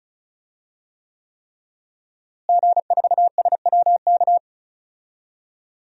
Morse code, heard as a keyed tone, about 36 wpm, 700 Hz. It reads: G4SWK
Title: DXCC-100 Call Signs - 5 Characters - Repeated 3x - 35wpm